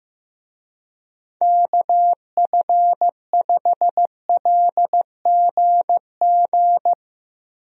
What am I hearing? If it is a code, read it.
Morse code, heard as a keyed tone, about 15 wpm, 700 Hz. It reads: KF5LGG
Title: USA Call Signs - 6 Characters - Repeated 2x - 15wpm